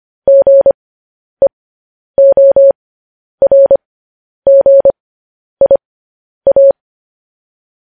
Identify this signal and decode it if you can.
Morse code, heard as a keyed tone, about 25 wpm, spaced as for 10 wpm, 550 Hz. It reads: GEORGIA
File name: podcast / News headlines-1x-25x10